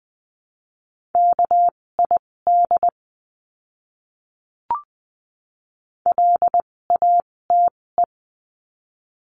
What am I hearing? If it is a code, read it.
Morse code, heard as a keyed tone, about 20 wpm, spaced as for 16 wpm, 700 Hz. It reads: KID LATE
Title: 256 Lesson 42a - Words 12-20wpm